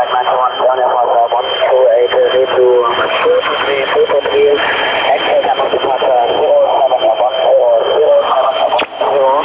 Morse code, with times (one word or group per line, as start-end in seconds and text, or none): none